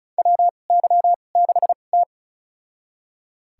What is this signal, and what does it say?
Morse code, heard as a keyed tone, about 34 wpm, spaced as for 25 wpm, 700 Hz. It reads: WY6T
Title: USA Call Signs - 4 Characters - 2x Character Spacing - 35wpm